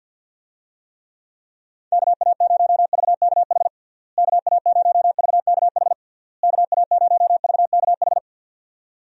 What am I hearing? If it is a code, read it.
Morse code, heard as a keyed tone, about 52 wpm, 700 Hz. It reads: KA0VKH KA0VKH KA0VKH
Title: USA Call Signs - 6 Characters - Repeated 3x - 50wpm